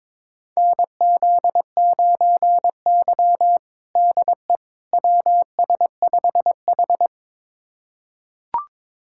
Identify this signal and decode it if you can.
Morse code, heard as a keyed tone, about 22 wpm, 700 Hz. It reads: NZ9Y DE WS5H
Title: USA Callsign DE Callsign - 4 Characters - 22wpm